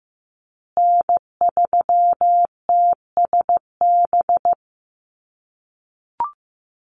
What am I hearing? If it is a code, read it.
Morse code, heard as a keyed tone, about 15 wpm, 700 Hz. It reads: N3TSB
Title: USA Call Signs Repeated 2x 15wpm